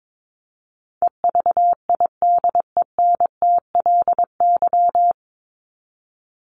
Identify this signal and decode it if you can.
Morse code, heard as a keyed tone, about 22 wpm, 700 Hz. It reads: EVIDENTLY
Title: LY Suffix - Top 100 Words with Base Words 22wpm